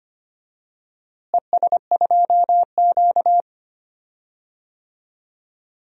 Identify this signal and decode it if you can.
Morse code, heard as a keyed tone, about 25 wpm, 700 Hz. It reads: ES2Q
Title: DXCC-100 Call Signs - 4 Characters - Repeated 2x - 25wpm